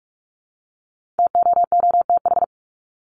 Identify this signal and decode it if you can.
Morse code, heard as a keyed tone, about 44 wpm, 700 Hz. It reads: TOOTH
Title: Top 1751-2000 Words 45wpm